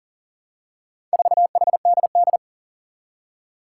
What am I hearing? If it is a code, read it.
Morse code, heard as a keyed tone, about 40 wpm, 700 Hz. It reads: 4HDD
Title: Four Challenging Letter-Number Combinations 40wpm